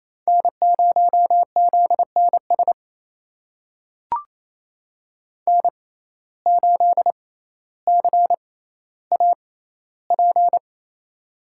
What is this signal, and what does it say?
Morse code, heard as a keyed tone, about 28 wpm, spaced as for 14 wpm, 700 Hz. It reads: N0ZNS N8CAP
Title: USA Call Signs - 5 Characters - 6x Character Spacing - 28wpm